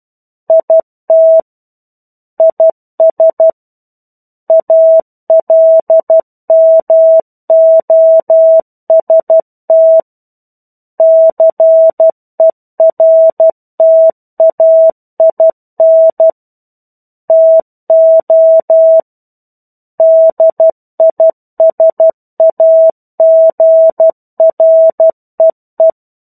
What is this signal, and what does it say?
Morse code, heard as a keyed tone, about 12 wpm, 650 Hz. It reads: IT IS ALMOST CERTAIN TO DISAGREE
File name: 01-Chapter-01 12 WPM